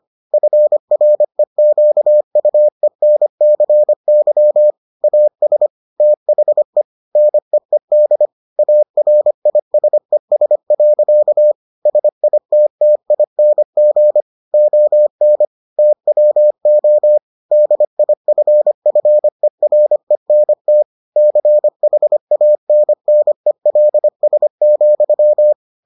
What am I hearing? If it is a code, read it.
Morse code, heard as a keyed tone, about 25 wpm, 600 Hz. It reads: FREQUENCY AS THE NEED ARISES. SITTING ON TWO DIFFERENT CHANNELS,